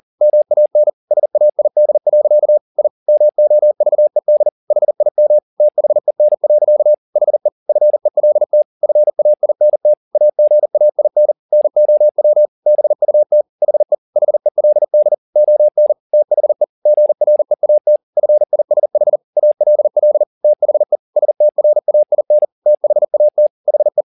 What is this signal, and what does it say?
Morse code, heard as a keyed tone, about 40 wpm, 600 Hz. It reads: MAN SAID. I MOVED HIM THEN. HE FELT FAINT AGAIN NOW BUT HE HELD ON THE GREAT FISH ALL THE STRAIN THAT HE
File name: the old man and the sea 2 40wpm